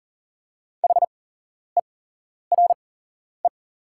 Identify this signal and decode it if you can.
Morse code, heard as a keyed tone, about 40 wpm, spaced as for 11 wpm, 700 Hz. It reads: HERE
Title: 8x Character Spacing - Top 200 Words - 40wpm